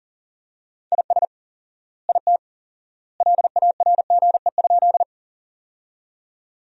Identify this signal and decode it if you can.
Morse code, heard as a keyed tone, about 38 wpm, 700 Hz. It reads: IS IT LARGE?
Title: 4x Word-Spacing - Sentences from Top 200 Words - 40wpm